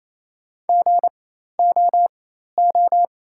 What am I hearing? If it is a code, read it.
Morse code, heard as a keyed tone, about 28 wpm, spaced as for 13 wpm, 690 Hz. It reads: GOO
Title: 4x Character Spacing - Top 200 Words - 28wpm